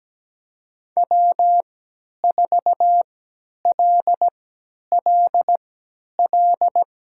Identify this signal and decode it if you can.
Morse code, heard as a keyed tone, about 17 wpm, spaced as for 10 wpm, 700 Hz. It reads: W4LLL
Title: USA Call Signs - 5 Characters - 3x Character Spacing - 17wpm